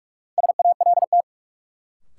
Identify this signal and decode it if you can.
Morse code, heard as a keyed tone, about 46 wpm, 700 Hz. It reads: SALT
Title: Sets of 5 Words - New Word Drill - 45wpm